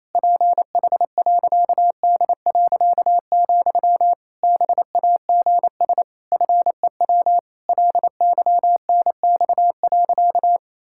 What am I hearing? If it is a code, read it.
Morse code, heard as a keyed tone, about 28 wpm, 700 Hz. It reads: PH.D., BAGS FEW LYNX.
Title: Pangram Challenge v2 - 28wpm